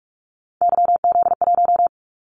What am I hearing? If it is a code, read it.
Morse code, heard as a keyed tone, about 46 wpm, 700 Hz. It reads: YZ1
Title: Callsign Prefixes for Top 25 DXCC - 45wpm